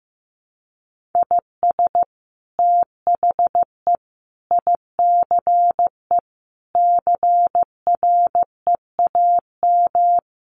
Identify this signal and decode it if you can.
Morse code, heard as a keyed tone, about 15 wpm, 700 Hz. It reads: IS THE ICE CREAM
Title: Sentences from Top 1500 Words 15wpm